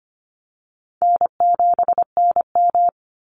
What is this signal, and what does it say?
Morse code, heard as a keyed tone, about 25 wpm, 700 Hz. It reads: N7NM
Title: Call Signs Repeated 3x - 25wpm